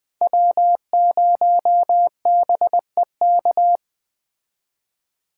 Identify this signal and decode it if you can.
Morse code, heard as a keyed tone, about 20 wpm, 700 Hz. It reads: W0BEK
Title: Speed Racing - USA 1x3 Call Signs 17wpm